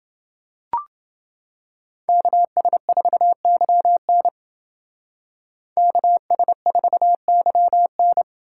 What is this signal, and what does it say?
Morse code, heard as a keyed tone, about 28 wpm, 700 Hz. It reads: KS4YN KS4YN